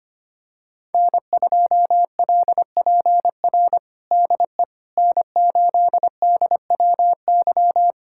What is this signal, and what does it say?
Morse code, heard as a keyed tone, about 25 wpm, 700 Hz. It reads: N2LPR DE N8DWY
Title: USA Callsign DE Callsign - 5 Characters - 25wpm